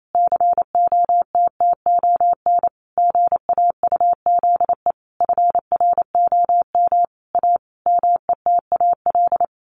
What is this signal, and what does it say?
Morse code, heard as a keyed tone, about 28 wpm, 700 Hz. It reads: COTTON GAUZE FROM A METAL
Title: A Sound Of Thunder 28wpm